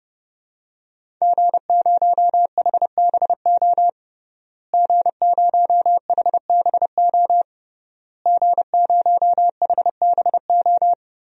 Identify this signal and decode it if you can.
Morse code, heard as a keyed tone, about 30 wpm, 700 Hz. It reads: G0HBO G0HBO G0HBO
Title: DXCC-100 Call Signs - 5 Characters - Repeated 3x - 30wpm